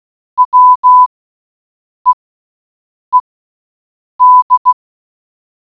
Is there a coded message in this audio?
Yes: Morse code from 0.4 to 4.7 s, heard as a keyed tone, about 16 wpm, spaced as for 7 wpm, 1000 Hz. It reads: WEED